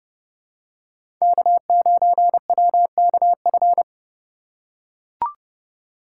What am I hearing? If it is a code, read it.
Morse code, heard as a keyed tone, about 30 wpm, 700 Hz. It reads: K9WKF
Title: USA Call Signs - 5 Characters - 2x Character Spacing - 30wpm